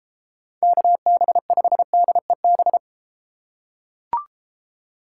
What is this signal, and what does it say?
Morse code, heard as a keyed tone, about 30 wpm, 700 Hz. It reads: KB5DEB